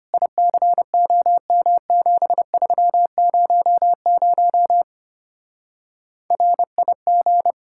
Morse code, heard as a keyed tone, about 32 wpm, 700 Hz. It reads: ICOM7300 RIG